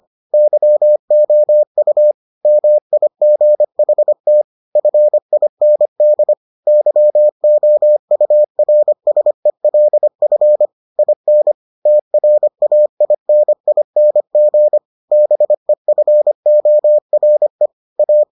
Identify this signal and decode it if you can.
Morse code, heard as a keyed tone, about 25 wpm, 600 Hz. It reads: YOU MIGHT FIND YOURSELF IN TRAINING BEFORE A